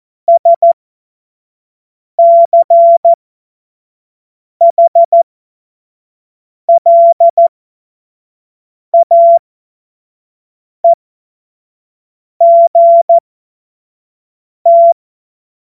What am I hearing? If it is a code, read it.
Morse code, heard as a keyed tone, about 14 wpm, spaced as for 5 wpm, 680 Hz. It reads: SCHLAEGT
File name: Morsekurs / Witze / Witz 19 Dezember 2023 25 F70